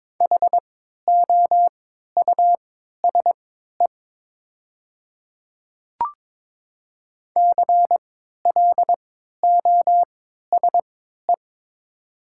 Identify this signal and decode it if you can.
Morse code, heard as a keyed tone, about 23 wpm, spaced as for 13 wpm, 700 Hz. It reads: HOUSE CLOSE
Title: 3x Character Spacing - Top 300 Words - 22wpm